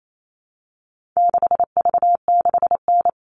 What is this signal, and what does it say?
Morse code, heard as a keyed tone, about 28 wpm, 700 Hz. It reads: 6V6N
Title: All Cut Numbers - 4-Digit 28wpm